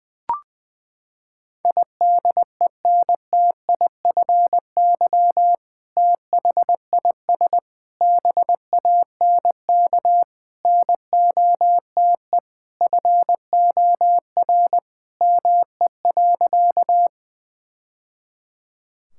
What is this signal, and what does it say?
Morse code, heard as a keyed tone, about 20 wpm, 700 Hz. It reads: IDENTIFY THIS BANK NOTE FOR ME.
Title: Sentences from Top 1000 Words - Review Part 6 - 20wpm